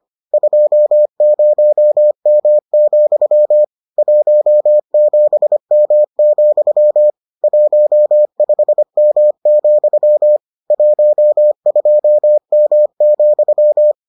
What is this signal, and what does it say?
Morse code, heard as a keyed tone, about 25 wpm, 600 Hz. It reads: 20M, 17M, 15M, 12M,